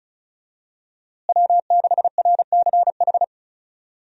Morse code, heard as a keyed tone, about 36 wpm, 700 Hz. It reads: W6RCH
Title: USA Call Signs - 5 Characters - Repeated 2x - 35wpm